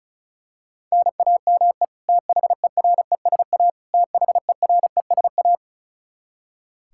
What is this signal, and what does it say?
Morse code, heard as a keyed tone, about 36 wpm, 700 Hz. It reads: NAME THERESA THERESA
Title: Practice QSO - Exchange 1 - Beginning - 35wpm